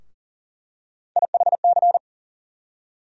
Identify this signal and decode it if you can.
Morse code, heard as a keyed tone, about 40 wpm, 700 Hz. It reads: IHC
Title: USA Call Sign Postfixes - 3 Letters 40wpm